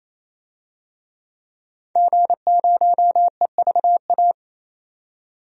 Morse code, heard as a keyed tone, about 27 wpm, 700 Hz. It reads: G0EVA
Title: DXCC-100 Call Signs - 5 Characters - 28wpm